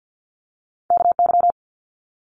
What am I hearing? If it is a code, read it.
Morse code, heard as a keyed tone, about 50 wpm, 700 Hz. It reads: KY
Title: Multi-letter Phonemes Quick Fire 50wpm